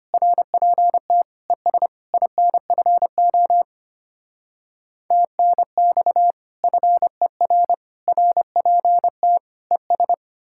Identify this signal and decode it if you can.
Morse code, heard as a keyed tone, about 33 wpm, 700 Hz. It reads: RPT ES INFO TNX FER RPT ES